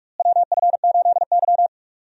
Morse code, heard as a keyed tone, about 42 wpm, 700 Hz. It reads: WF8Y
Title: USA Call Signs - 4 Characters - Repeated 2x - 45wpm